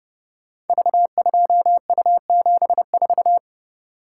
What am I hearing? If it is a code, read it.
Morse code, heard as a keyed tone, about 30 wpm, 700 Hz. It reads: V2U74
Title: Code Groups - 5 Letter:Numbers - 30wpm